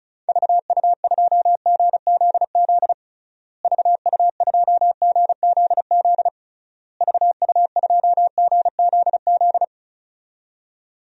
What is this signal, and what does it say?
Morse code, heard as a keyed tone, about 36 wpm, 700 Hz. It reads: VU2GZZ VU2GZZ VU2GZZ
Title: DXCC-100 Call Signs - 6 Characters - Repeated 3x - 35wpm